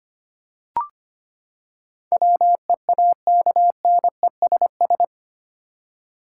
Morse code, heard as a keyed tone, about 25 wpm, 700 Hz. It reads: WEAKNESS
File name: Eight Letter Words - 25wpm